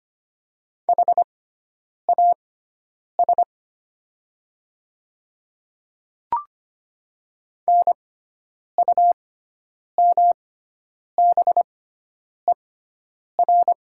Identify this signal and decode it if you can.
Morse code, heard as a keyed tone, about 25 wpm, spaced as for 9 wpm, 700 Hz. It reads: HAS NUMBER
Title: 6x Character Spacing - Top 100 Words - 25wpm